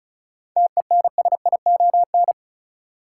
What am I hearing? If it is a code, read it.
Morse code, heard as a keyed tone, about 36 wpm, 700 Hz. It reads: TENSION
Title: Sentences from Top 1600 Words - New Word Drill - 35wpm